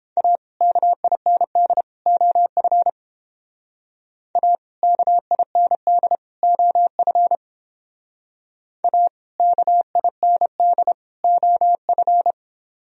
Morse code, heard as a keyed tone, about 30 wpm, 700 Hz. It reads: A KIND OF A KIND OF A KIND OF